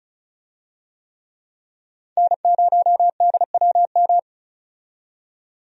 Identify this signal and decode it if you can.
Morse code, heard as a keyed tone, about 36 wpm, 700 Hz. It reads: N0DWM
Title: USA Call Signs - 5 Characters - 35wpm